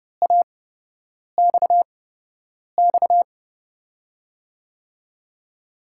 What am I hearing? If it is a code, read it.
Morse code, heard as a keyed tone, about 32 wpm, spaced as for 8 wpm, 700 Hz. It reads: AXX
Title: USA Call Signs - 6 Characters - 8x Character Spacing - 30wpm